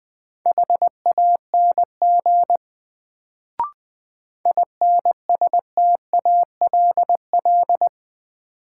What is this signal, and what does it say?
Morse code, heard as a keyed tone, about 20 wpm, 700 Hz. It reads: HANG INSTALL
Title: Sentences from Top 1600 Words - New Word Drill Alt - 20wpm